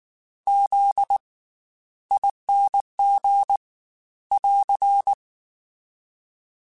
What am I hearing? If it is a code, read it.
Morse code, heard as a keyed tone, about 19 wpm, 800 Hz. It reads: Z ING <AR>
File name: Special INGs / Special INGs-mp3s / Slow ing @18